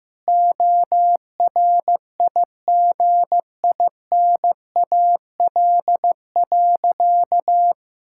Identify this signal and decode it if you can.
Morse code, heard as a keyed tone, about 15 wpm, 700 Hz. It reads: ORIGINAL.
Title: Sentences from Top 1000 Words - Review Part 6 - 15wpm